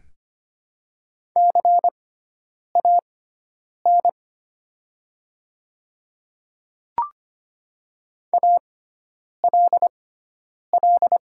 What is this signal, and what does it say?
Morse code, heard as a keyed tone, about 25 wpm, spaced as for 9 wpm, 700 Hz. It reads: CAN ALL
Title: 6x Character Spacing - Top 100 Words - 25wpm